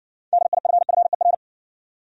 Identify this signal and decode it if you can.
Morse code, heard as a keyed tone, about 60 wpm, 700 Hz. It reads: DIFFER